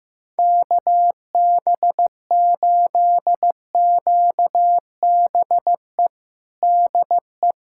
Morse code, heard as a keyed tone, about 16 wpm, 700 Hz. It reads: KB8QBE DE
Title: USA Callsign DE Callsign 15wpm